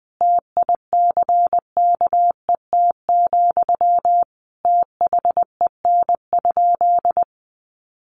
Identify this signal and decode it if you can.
Morse code, heard as a keyed tone, about 19 wpm, 700 Hz. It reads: TICKET, THEN?
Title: Sentences from Top 1400 Words 20wpm